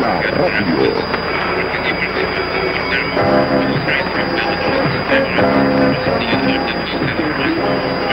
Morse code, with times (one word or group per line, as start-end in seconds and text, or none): none